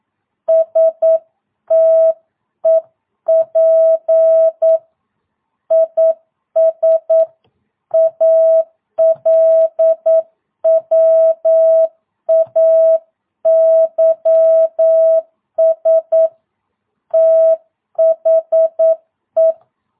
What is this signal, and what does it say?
Morse code, heard as a keyed tone, about 9 wpm, 650 Hz. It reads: STEP IS ALWAYS THE
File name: jh8cbh / 2022cw